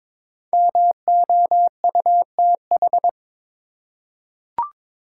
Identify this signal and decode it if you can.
Morse code, heard as a keyed tone, about 22 wpm, 700 Hz. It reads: MOUTH
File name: 5 Letter Words - First 3 Letters Given - 22wpm